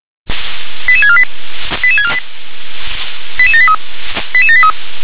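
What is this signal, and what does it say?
Five-tone selective call (ZVEI), heard as quick runs of five tones, tones of about 70 ms. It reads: 88648 88648 88783 88783
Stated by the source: ZVEI five-tone call 88633 received on SDR.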